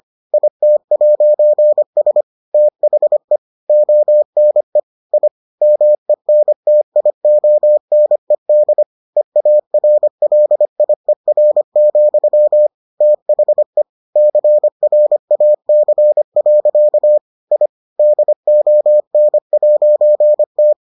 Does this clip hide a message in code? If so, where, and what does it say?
Yes: Morse code from 0.3 to 20.7 s, heard as a keyed tone, about 25 wpm, 600 Hz. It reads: IT'S THE ONE I MENTIONED EARLIER, THE CRAC. I DON'T